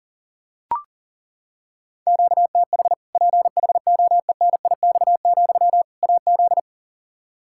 Morse code, heard as a keyed tone, about 40 wpm, 700 Hz. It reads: QTH PHOENIX, AZ
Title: Practice QSO - Exchange 1 - Beginning - 40wpm